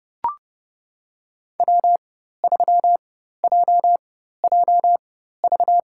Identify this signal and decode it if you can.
Morse code, heard as a keyed tone, about 30 wpm, spaced as for 14 wpm, 700 Hz. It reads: W3JJV
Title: USA Call Signs - 5 Characters - 4x Character Spacing - 30wpm